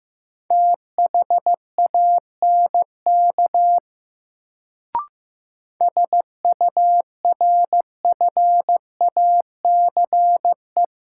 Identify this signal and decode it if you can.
Morse code, heard as a keyed tone, about 15 wpm, 700 Hz. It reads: THANK SURFACE
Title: Sentences from Top 600 Words - New Word Drill - 15wpm